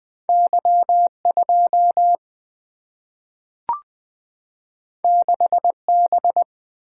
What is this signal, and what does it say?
Morse code, heard as a keyed tone, about 20 wpm, 700 Hz. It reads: Y 2 6 B
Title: Two Letter-Number Combinations - 20wpm